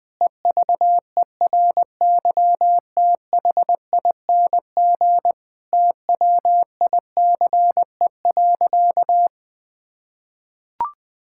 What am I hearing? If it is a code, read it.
Morse code, heard as a keyed tone, about 21 wpm, 700 Hz. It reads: EVERYTHING TWICE.
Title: Sentences from Top 1600 Words 20wpm